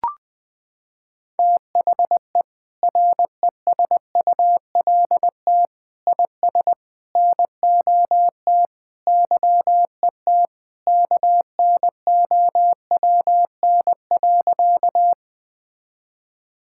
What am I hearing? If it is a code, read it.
Morse code, heard as a keyed tone, about 20 wpm, 700 Hz. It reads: THE RESULT IS NOT YET KNOWN.
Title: Sentences from Top 1000 Words - Review Part 5 - 20wpm